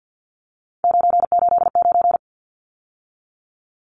Morse code, heard as a keyed tone, about 49 wpm, 700 Hz. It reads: 989
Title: Numbers - 3-Digit 50wpm